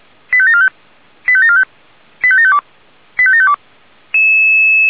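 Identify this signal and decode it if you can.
Five-tone selective call (ZVEI), heard as quick runs of five tones, tones of about 70 ms. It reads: 85745 85745 85752 85752
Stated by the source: ZVEI five-tone selective call received by SDR, from the feuerwehr-orlamuende.org audio archive.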